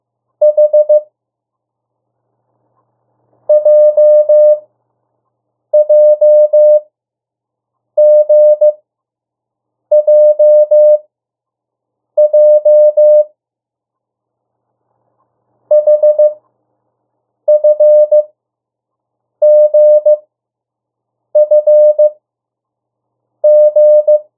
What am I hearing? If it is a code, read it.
Morse code, heard as a keyed tone, about 15 wpm, spaced as for 6 wpm, 600 Hz. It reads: H JJGJJ HFGFG
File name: urok 1